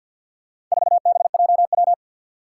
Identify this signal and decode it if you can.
Morse code, heard as a keyed tone, about 47 wpm, 700 Hz. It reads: 4BJW
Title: Four Challenging Letter-Number Combinations 50wpm